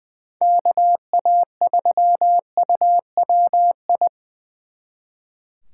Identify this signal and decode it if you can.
Morse code, heard as a keyed tone, about 24 wpm, 700 Hz. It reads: KA3UWI
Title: Speed Racing - USA 2x3 Call Signs 20wpm